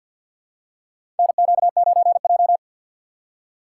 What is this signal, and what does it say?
Morse code, heard as a keyed tone, about 48 wpm, 700 Hz. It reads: NQ9J
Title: USA Call Signs - 4 Characters - 6x Character Spacing - 50wpm